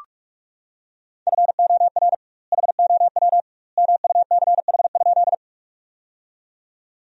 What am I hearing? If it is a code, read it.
Morse code, heard as a keyed tone, about 46 wpm, 700 Hz. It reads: FOR HOW MUCH?
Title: 2x Word-Spacing - Sentences from Top 200 Words - 45wpm